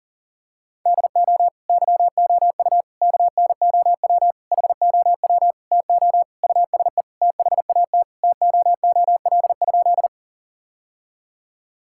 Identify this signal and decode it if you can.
Morse code, heard as a keyed tone, about 40 wpm, 700 Hz. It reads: DO YOU KNOW HOW TO USE THAT TOOL?
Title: Sentences from Top 1000 Words - Review Part 6 - 40wpm